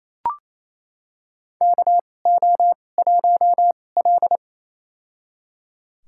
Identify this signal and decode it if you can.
Morse code, heard as a keyed tone, about 28 wpm, spaced as for 20 wpm, 700 Hz. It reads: KO1L
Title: USA Call Signs - 4 Characters - 2x Character Spacing - 28wpm